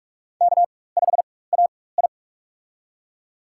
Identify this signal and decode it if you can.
Morse code, heard as a keyed tone, about 46 wpm, spaced as for 21 wpm, 700 Hz. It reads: K5AI
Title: USA Call Signs - 4 Characters - 4x Character Spacing - 45wpm